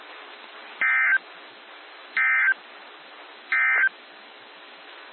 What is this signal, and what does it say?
Emergency Alert System (SAME) tones, heard as harsh buzzing bursts. It is heard from 0.7 to 4.0 s.
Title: NOAA Weather Radio - WXJ73